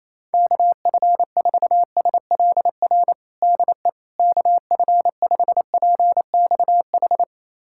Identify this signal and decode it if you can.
Morse code, heard as a keyed tone, about 28 wpm, 700 Hz. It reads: KF4SLR DE KF5PXH
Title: USA Callsign DE Callsign - 6 Characters - 28wpm